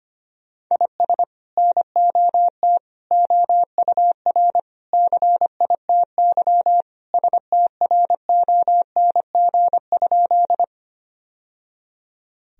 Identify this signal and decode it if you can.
Morse code, heard as a keyed tone, about 25 wpm, 700 Hz. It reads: IS NOT OUR CITY STRONG?
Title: Sentences from Top 1000 Words - Review Part 3 - 25wpm